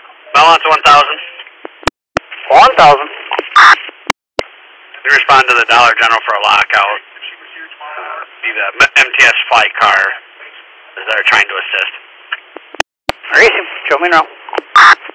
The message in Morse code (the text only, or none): none